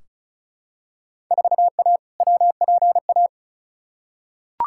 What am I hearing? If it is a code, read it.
Morse code, heard as a keyed tone, about 36 wpm, 700 Hz. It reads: 4A WPA
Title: Field Day Exchanges Encore - 35wpm